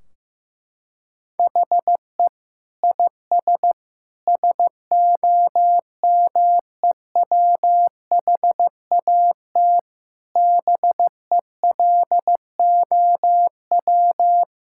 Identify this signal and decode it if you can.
Morse code, heard as a keyed tone, about 15 wpm, 700 Hz. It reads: HE IS SOMEWHAT BELOW
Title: Sentences from Top 1600 Words 15wpm